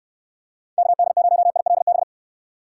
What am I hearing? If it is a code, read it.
Morse code, heard as a keyed tone, about 67 wpm, 700 Hz. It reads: KD0ILB